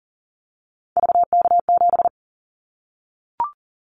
Morse code, heard as a keyed tone, about 38 wpm, 700 Hz. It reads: VK7
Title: Callsign Prefixes for Top 50 DXCC - 40wpm